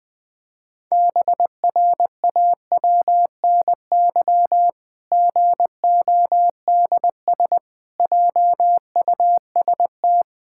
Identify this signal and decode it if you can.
Morse code, heard as a keyed tone, about 20 wpm, 700 Hz. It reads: BRAWNY GODS JUST
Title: Pangram Challenge v2 - 20wpm